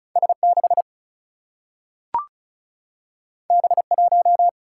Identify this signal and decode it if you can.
Morse code, heard as a keyed tone, about 36 wpm, 700 Hz. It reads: S6 B1
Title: Single-Letter Single-Number 35wpm